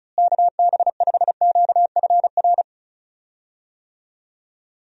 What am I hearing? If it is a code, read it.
Morse code, heard as a keyed tone, about 36 wpm, 700 Hz. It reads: KB5QFR
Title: USA Call Signs - 6 Characters - Repeated 3x - 35wpm